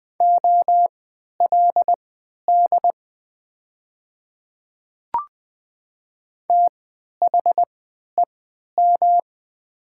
Morse code, heard as a keyed tone, about 20 wpm, spaced as for 11 wpm, 700 Hz. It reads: OLD THEM
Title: 3x Character Spacing - Top 200 Words - 20wpm